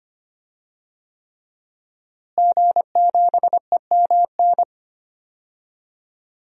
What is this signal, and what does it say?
Morse code, heard as a keyed tone, about 25 wpm, 700 Hz. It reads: G7EMN
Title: DXCC-100 Call Signs - 5 Characters - Repeated 3x - 25wpm